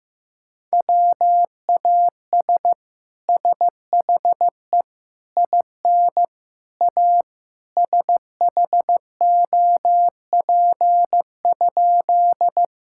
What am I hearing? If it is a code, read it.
Morse code, heard as a keyed tone, about 15 wpm, 700 Hz. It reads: WAS SHE IN A SHOP?
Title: Sentences from Top 1600 Words 15wpm